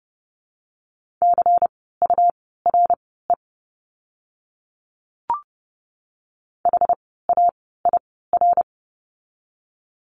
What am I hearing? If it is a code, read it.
Morse code, heard as a keyed tone, about 30 wpm, spaced as for 17 wpm, 700 Hz. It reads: CURE HAIR
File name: Lesson 17f - Letter u - Words 30wpm